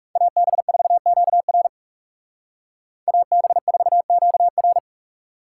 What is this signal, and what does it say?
Morse code, heard as a keyed tone, about 38 wpm, 700 Hz. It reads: AB4QR AB4QR
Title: Speed Racing - USA Call Signs 30wpm